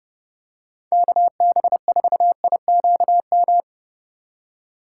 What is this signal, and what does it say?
Morse code, heard as a keyed tone, about 31 wpm, 700 Hz. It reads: KB4IQM